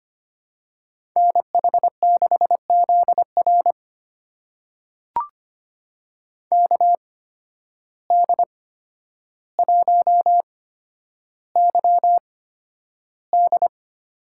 Morse code, heard as a keyed tone, about 25 wpm, spaced as for 11 wpm, 700 Hz. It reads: NH6ZR KD1YD
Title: USA Call Signs - 5 Characters - 8x Character Spacing - 25wpm